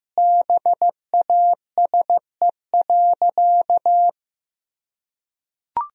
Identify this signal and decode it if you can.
Morse code, heard as a keyed tone, about 15 wpm, 700 Hz. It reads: BASE.
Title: Sentences from Top 1000 Words - Review Part 1 - 15wpm